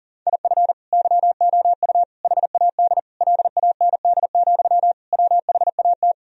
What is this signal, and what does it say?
Morse code, heard as a keyed tone, about 40 wpm, 700 Hz. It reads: IF YOU HAD LAND, WHAT
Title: Sentences from Top 1000 Words - Review Part 1 - 40wpm